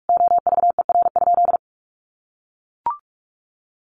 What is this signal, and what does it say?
Morse code, heard as a keyed tone, about 44 wpm, 700 Hz. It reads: OVER?
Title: Sentences from Top 1200 Words 45wpm